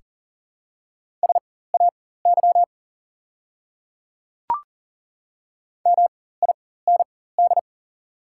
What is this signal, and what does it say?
Morse code, heard as a keyed tone, about 40 wpm, spaced as for 19 wpm, 700 Hz. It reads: SAY MIND
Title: 4x Character Spacing - Top 500 Words - 40wpm